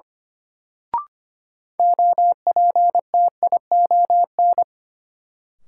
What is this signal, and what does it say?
Morse code, heard as a keyed tone, about 25 wpm, 700 Hz. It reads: OPTION
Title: Sentences from Top 1300 Words - New Word Drill - 25wpm